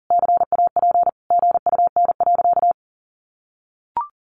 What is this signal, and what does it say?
Morse code, heard as a keyed tone, about 37 wpm, 700 Hz. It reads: CAP GUN.
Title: A Sound Of Thunder 40wpm